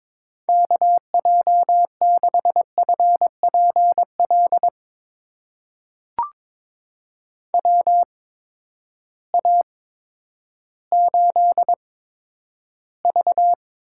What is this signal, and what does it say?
Morse code, heard as a keyed tone, about 21 wpm, spaced as for 11 wpm, 700 Hz. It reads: KJ6FPL WA8V
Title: USA Call Signs - 6 Characters - 8x Character Spacing - 22wpm